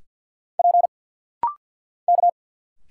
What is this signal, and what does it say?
Morse code, heard as a keyed tone, about 50 wpm, 700 Hz. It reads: P K
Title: Single Letters - Rapid-Fire 50wpm